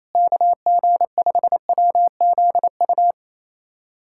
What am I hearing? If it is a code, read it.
Morse code, heard as a keyed tone, about 28 wpm, 700 Hz. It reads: KG5WZU
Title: USA Callsign DE Callsign - 6 Characters - 28wpm